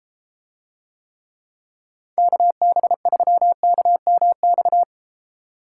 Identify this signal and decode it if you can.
Morse code, heard as a keyed tone, about 32 wpm, 700 Hz. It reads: KB3KMX